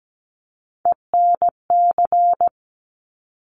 Morse code, heard as a keyed tone, about 17 wpm, 700 Hz. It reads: ENC
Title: Top Three Letter Combinations in Words 17wpm